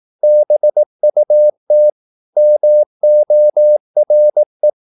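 Morse code, heard as a keyed tone, about 18 wpm, 600 Hz. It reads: BUT MORE
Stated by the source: Morse practice at 18 wpm, English text, from CW med Gnister.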